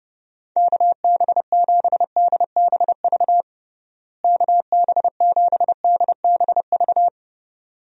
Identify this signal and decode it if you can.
Morse code, heard as a keyed tone, about 30 wpm, 700 Hz. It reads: KB7DBV KB7DBV
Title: USA Call Signs - 6 Characters - Repeated 2x - 30wpm